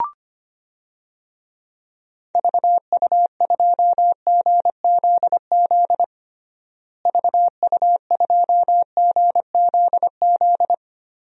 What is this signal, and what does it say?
Morse code, heard as a keyed tone, about 25 wpm, 700 Hz. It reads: VU2GZZ VU2GZZ
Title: DXCC-100 Call Signs - 6 Characters - Repeated 3x - 25wpm